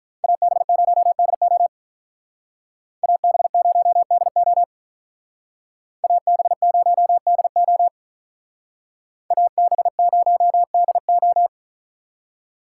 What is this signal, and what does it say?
Morse code, heard as a keyed tone, about 43 wpm, 700 Hz. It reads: AB0DO AB0DO AB0DO AB0DO
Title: Speed Racing - USA 2x2 Call Signs 35wpm